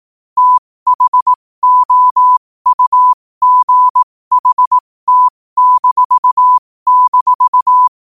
Morse code, heard as a keyed tone, about 18 wpm, spaced as for 15 wpm, 1000 Hz. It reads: THOUGHT--
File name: qotdfiles / 15wpm